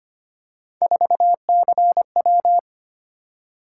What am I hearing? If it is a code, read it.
Morse code, heard as a keyed tone, about 25 wpm, 700 Hz. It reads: 4CW
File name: Three Letter-Number Combinations - 25wpm